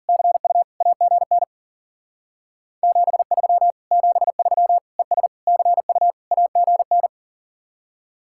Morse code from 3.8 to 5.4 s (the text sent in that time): 73 ES